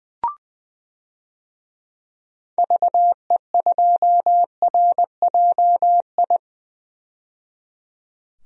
Morse code, heard as a keyed tone, about 20 wpm, 700 Hz. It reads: VE2RJI
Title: DXCC-100 Call Signs - 6 Characters - 20wpm